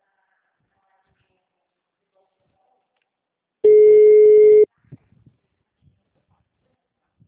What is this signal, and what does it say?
Ringback tone (Europe, 425 Hz).